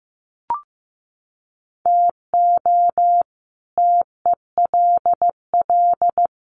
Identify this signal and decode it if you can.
Morse code, heard as a keyed tone, about 15 wpm, 700 Hz. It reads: TO TELL
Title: Sets of 3 Words 15wpm